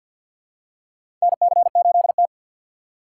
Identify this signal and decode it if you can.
Morse code, heard as a keyed tone, about 48 wpm, 700 Hz. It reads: NC8T
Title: USA Call Signs - 4 Characters - 3x Character Spacing - 50wpm